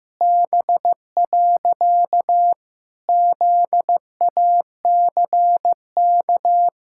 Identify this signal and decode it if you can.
Morse code, heard as a keyed tone, about 16 wpm, 700 Hz. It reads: B. ZACK
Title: Pangram Challenge v2 - 15wpm